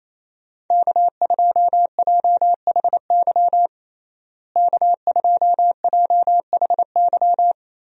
Morse code, heard as a keyed tone, about 28 wpm, 700 Hz. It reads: K2JHY K2JHY
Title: USA Call Signs - 5 Characters - Repeated 2x - 28wpm